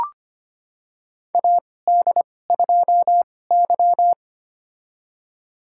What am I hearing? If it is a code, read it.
Morse code, heard as a keyed tone, about 25 wpm, spaced as for 18 wpm, 700 Hz. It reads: AD2Y
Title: USA Call Signs - 4 Characters - 2x Character Spacing - 25wpm